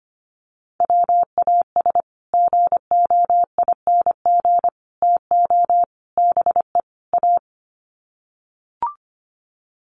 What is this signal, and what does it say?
Morse code, heard as a keyed tone, about 25 wpm, 700 Hz. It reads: WAS GOING TO BE A
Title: Sets of 5 Words 25wpm